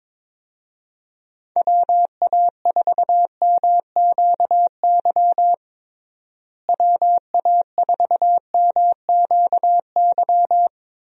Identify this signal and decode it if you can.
Morse code, heard as a keyed tone, about 22 wpm, 700 Hz. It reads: WA4MQY WA4MQY
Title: USA Call Signs - 6 Characters - Repeated 3x - 22wpm